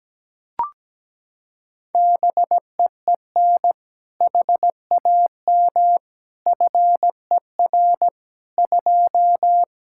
Morse code, heard as a keyed tone, about 17 wpm, 700 Hz. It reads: BEEN HAM FER 2
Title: Practice QSO - Exchange 3 - More Info - 17wpm